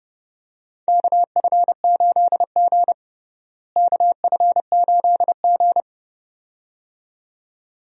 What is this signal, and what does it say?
Morse code, heard as a keyed tone, about 32 wpm, 700 Hz. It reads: KF8G KF8G
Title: USA Call Signs - 4 Characters - Repeated 2x - 30wpm